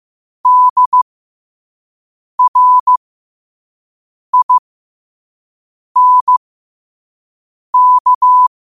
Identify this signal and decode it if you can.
Morse code, heard as a keyed tone, about 15 wpm, spaced as for 5 wpm, 1000 Hz. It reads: DRINK
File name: qotdfiles / 5wpm